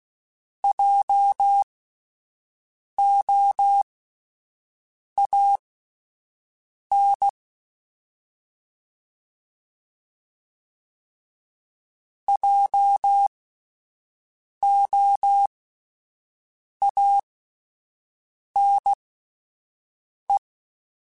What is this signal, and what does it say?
Morse code, heard as a keyed tone, about 16 wpm, spaced as for 5 wpm, 800 Hz. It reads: JOAN JOANE